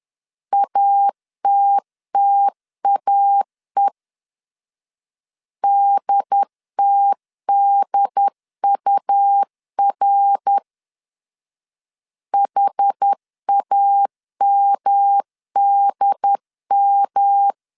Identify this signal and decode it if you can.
Morse code, heard as a keyed tone, about 10 wpm, 800 Hz. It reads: ATTAE DTDUR HAMDM
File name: CW / DET / 1 3 1500C S C 10wpm 3335